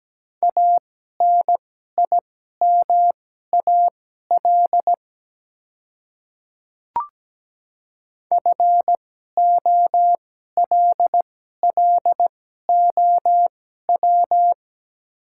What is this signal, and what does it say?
Morse code, heard as a keyed tone, about 17 wpm, spaced as for 12 wpm, 700 Hz. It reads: ANIMAL FOLLOW
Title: 2x Character Spacing - Top 300 Words - 17wpm